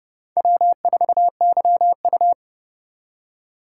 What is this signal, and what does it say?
Morse code, heard as a keyed tone, about 28 wpm, 700 Hz. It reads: W4YU